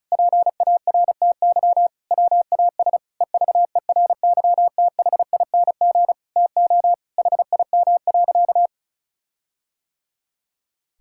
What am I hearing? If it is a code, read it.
Morse code, heard as a keyed tone, about 36 wpm, 700 Hz. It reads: PARTY WAS EVERYTHING TO HIM.
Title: Sentences from Top 1600 Words 35wpm